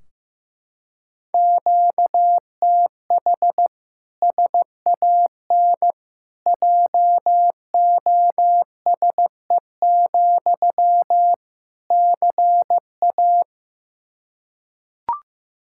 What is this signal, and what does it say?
Morse code, heard as a keyed tone, about 16 wpm, 700 Hz. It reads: QTH SAN JOSE, CA
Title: Practice QSO - Exchange 1 - Beginning - 15wpm